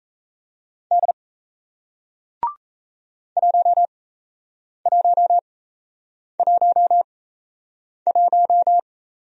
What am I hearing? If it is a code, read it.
Morse code, heard as a keyed tone, about 38 wpm, spaced as for 8 wpm, 700 Hz. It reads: D 1111